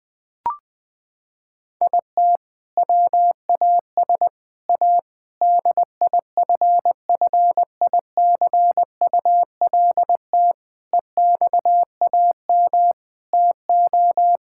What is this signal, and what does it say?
Morse code, heard as a keyed tone, about 21 wpm, 700 Hz. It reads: IT WAS A DIFFICULT EXAM TO
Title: Sentences from Top 1000 Words - Review Part 5 - 20wpm